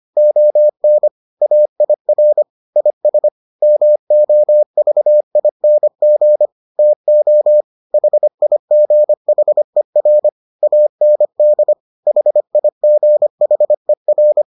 Morse code, heard as a keyed tone, about 25 wpm, 600 Hz. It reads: ON AIR IS MOVING TO HIGHER AND HIGHER